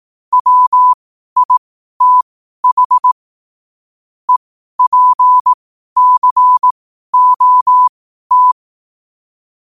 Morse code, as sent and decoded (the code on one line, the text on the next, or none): .-- .. - .... / . .--. -.-. --- -
WITH EPCOT